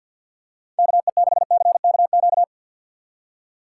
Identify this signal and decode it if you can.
Morse code, heard as a keyed tone, about 52 wpm, 700 Hz. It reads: KE6CKQ
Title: USA Call Signs - 6 Characters - 6x Character Spacing - 50wpm